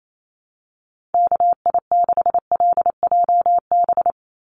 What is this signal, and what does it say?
Morse code, heard as a keyed tone, about 28 wpm, 700 Hz. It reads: KI6LJB
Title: USA Call Signs 28wpm